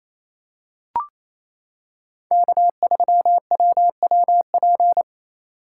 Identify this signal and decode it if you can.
Morse code, heard as a keyed tone, about 28 wpm, 700 Hz. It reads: K3WWP
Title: K1USN SST QSO Practice - 28wpm